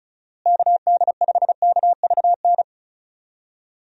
Morse code, heard as a keyed tone, about 34 wpm, 700 Hz. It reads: KD5KVN
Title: USA Call Signs - 6 Characters - Repeated 3x - 35wpm